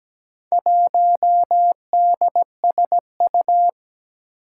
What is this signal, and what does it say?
Morse code, heard as a keyed tone, about 17 wpm, 700 Hz. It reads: 1DSU
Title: Four Challenging Letter-Number Combinations 17wpm